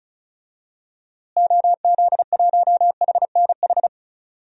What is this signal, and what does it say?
Morse code, heard as a keyed tone, about 36 wpm, 700 Hz. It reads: OZ1HNH
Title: DXCC-100 Call Signs - 6 Characters - Repeated 3x - 35wpm